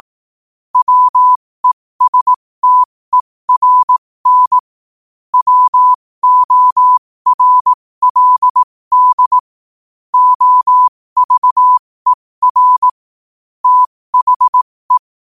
Morse code, heard as a keyed tone, about 18 wpm, spaced as for 15 wpm, 1000 Hz. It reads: WESTERN WORLD OVER THE
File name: qotdfiles / 15wpm